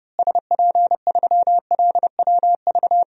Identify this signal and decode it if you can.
Morse code, heard as a keyed tone, about 32 wpm, 700 Hz. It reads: SP3LWV
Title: DXCC-100 Call Signs - 6 Characters - Repeated 2x - 30wpm